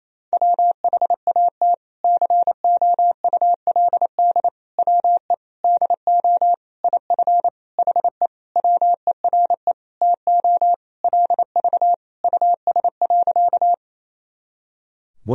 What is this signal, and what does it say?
Morse code, heard as a keyed tone, about 28 wpm, 700 Hz. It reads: WHAT COULD WE DO IF HE WERE TO LV US.
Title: Sentences from Top 400 Words and Abbreviations 28wpm